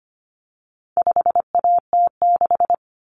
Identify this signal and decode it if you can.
Morse code, heard as a keyed tone, about 25 wpm, 700 Hz. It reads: 5AT6
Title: Common Cut Numbers - 4-Digit 25wpm